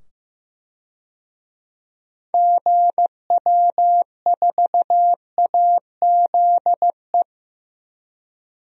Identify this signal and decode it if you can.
Morse code, heard as a keyed tone, about 15 wpm, 700 Hz. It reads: GW4AZE
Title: DXCC-100 Call Signs - 6 Characters - 15wpm